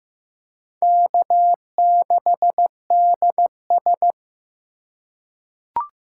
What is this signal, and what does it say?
Morse code, heard as a keyed tone, about 15 wpm, 700 Hz. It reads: K6DS
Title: USA Call Signs - 2x Character Spacing - 15wpm